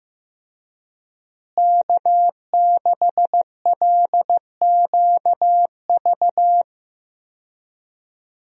Morse code, heard as a keyed tone, about 15 wpm, 700 Hz. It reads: K6LQV
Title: USA Call Signs - 5 Characters - Repeated 2x - 15wpm